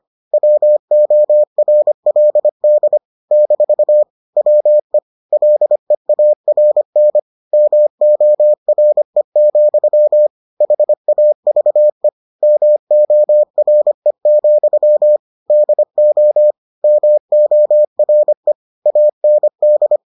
Morse code, heard as a keyed tone, about 25 wpm, 600 Hz. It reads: WORLD - WE LEARN MORE, HAVE MORE, DO MORE AND